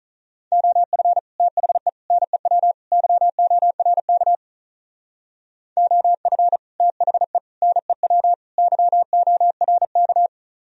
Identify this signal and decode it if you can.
Morse code, heard as a keyed tone, about 42 wpm, 700 Hz. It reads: OF THE NEW YORK OF THE NEW YORK
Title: Speed-Racing - Sets of 4 Words 35wpm